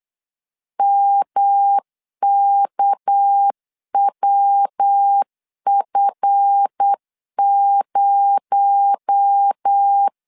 Morse code, heard as a keyed tone, about 8 wpm, 800 Hz. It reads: MKWF0
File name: LEZIONI / 7 GF9Z / 7 GF9 8wpm 3545